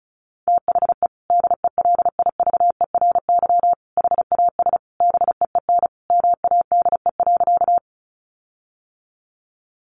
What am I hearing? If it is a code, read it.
Morse code, heard as a keyed tone, about 34 wpm, 700 Hz. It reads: THE DELIVERY HAS BEEN MADE.
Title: Sentences from Top 1500 Words 35wpm